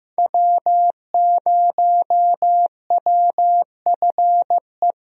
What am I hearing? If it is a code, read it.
Morse code, heard as a keyed tone, about 15 wpm, 700 Hz. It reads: W0WFE
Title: USA Call Signs - 5 Characters - Repeated 3x - 15wpm